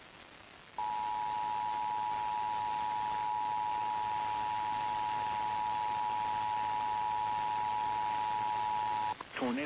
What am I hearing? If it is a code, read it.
Emergency Alert System (SAME) tones (the 853 and 960 Hz attention tone), with speech. It is heard from 0.8 to 9.2 s.
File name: eas / KKPRKXPN / KKPRKXPN